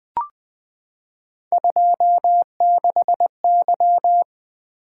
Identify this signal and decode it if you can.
Morse code, heard as a keyed tone, about 20 wpm, 700 Hz. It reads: 26Y
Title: Three Letter-Number Combinations - 20wpm